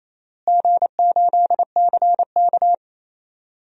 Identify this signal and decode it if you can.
Morse code, heard as a keyed tone, about 28 wpm, 700 Hz. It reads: G8CK
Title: DXCC-100 Call Signs - 4 Characters - Repeated 3x - 28wpm